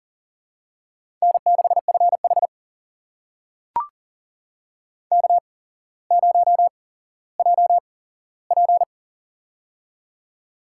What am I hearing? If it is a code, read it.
Morse code, heard as a keyed tone, about 40 wpm, spaced as for 17 wpm, 700 Hz. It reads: N6FH K0JP
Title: USA Call Signs - 4 Characters - 8x Character Spacing - 40wpm